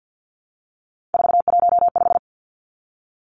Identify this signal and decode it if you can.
Morse code, heard as a keyed tone, about 48 wpm, 700 Hz. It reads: 415
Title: Numbers - 3-Digit 50wpm